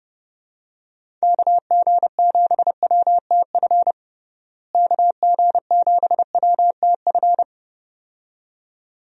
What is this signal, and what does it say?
Morse code, heard as a keyed tone, about 30 wpm, 700 Hz. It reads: KG7WTF KG7WTF
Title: USA Call Signs - 6 Characters - Repeated 2x - 30wpm